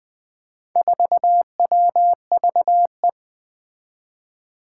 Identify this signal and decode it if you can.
Morse code, heard as a keyed tone, about 20 wpm, 700 Hz. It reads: 4WVE
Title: Four Challenging Letter-Number Combinations 20wpm